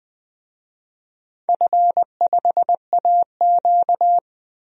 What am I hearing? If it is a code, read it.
Morse code, heard as a keyed tone, about 20 wpm, 700 Hz. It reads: F5AQ
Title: DXCC-100 Call Signs - 4 Characters - 20wpm